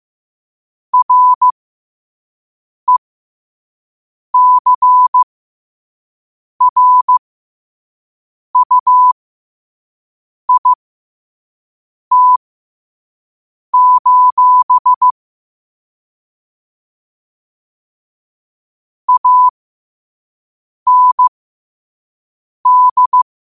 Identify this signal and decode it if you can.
Morse code, heard as a keyed tone, about 15 wpm, spaced as for 5 wpm, 1000 Hz. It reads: RECRUIT: AND
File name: qotdfiles / 5wpm